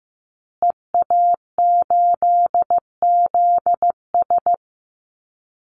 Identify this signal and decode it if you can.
Morse code, heard as a keyed tone, about 15 wpm, 700 Hz. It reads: EA8ZS
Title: Call Signs Repeated 2x - 15wpm